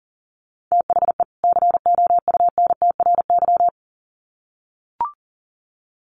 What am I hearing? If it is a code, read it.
Morse code, heard as a keyed tone, about 40 wpm, 700 Hz. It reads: THE COUNTRY
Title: Sets of 2 Words 40wpm